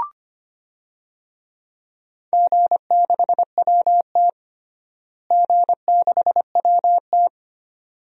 Morse code, heard as a keyed tone, about 25 wpm, 700 Hz. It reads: G6WT G6WT
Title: DXCC-100 Call Signs - 4 Characters - Repeated 2x - 25wpm